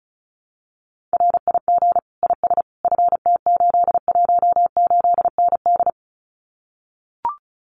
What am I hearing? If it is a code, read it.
Morse code, heard as a keyed tone, about 36 wpm, 700 Hz. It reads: RIG IS FT818ND
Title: Common QSO Phrases Encore 35wpm